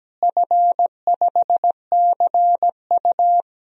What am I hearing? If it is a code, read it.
Morse code, heard as a keyed tone, about 17 wpm, 700 Hz. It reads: F5CU
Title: DXCC-100 Call Signs - 4 Characters - Repeated 3x - 17wpm